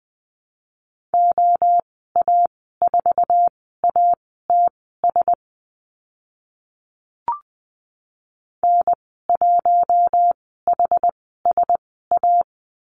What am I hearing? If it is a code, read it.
Morse code, heard as a keyed tone, about 20 wpm, spaced as for 15 wpm, 700 Hz. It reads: OA4ATS N1HSA
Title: 069 Lesson 12g - Letter d - Call Signs 20wpm